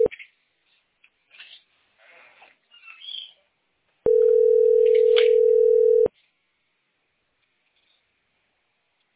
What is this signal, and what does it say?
Ringback tone (North America), 440 + 480 Hz.